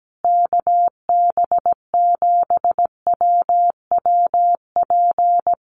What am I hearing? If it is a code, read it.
Morse code, heard as a keyed tone, about 17 wpm, 700 Hz. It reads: KB7WWP
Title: Common QSO Phrases Encore 17wpm